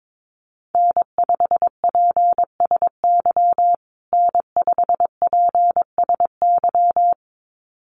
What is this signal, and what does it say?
Morse code, heard as a keyed tone, about 22 wpm, 700 Hz. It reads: N5PSY N5PSY
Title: USA Call Signs Repeated 2x 22wpm